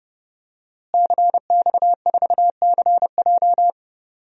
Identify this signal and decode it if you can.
Morse code, heard as a keyed tone, about 30 wpm, 700 Hz. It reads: CX4CJ
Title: DXCC-100 Call Signs - 5 Characters - Repeated 3x - 30wpm